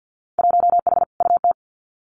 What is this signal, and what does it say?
Morse code, heard as a keyed tone, about 50 wpm, 700 Hz. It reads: 1H UT
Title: Winter Field Day Exchanges 50wpm